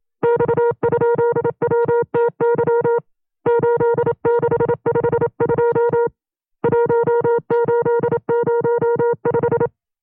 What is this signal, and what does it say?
Morse code, heard as a keyed tone, about 27 wpm, 470 Hz. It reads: X?WTY 8652 1805
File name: 25WPM